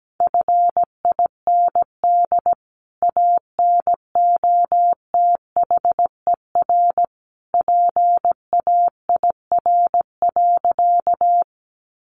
Morse code, heard as a keyed tone, about 17 wpm, 700 Hz. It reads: FIND ANOTHER PAIR.
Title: Sentences from Top 1200 Words 17wpm